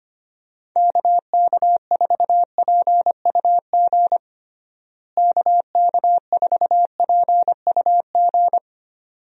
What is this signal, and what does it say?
Morse code, heard as a keyed tone, about 25 wpm, 700 Hz. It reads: KK4PUG KK4PUG
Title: USA Call Signs - 6 Characters - Repeated 2x - 25wpm